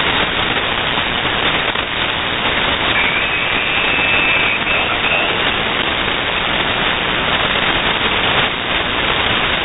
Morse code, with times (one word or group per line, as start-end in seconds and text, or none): none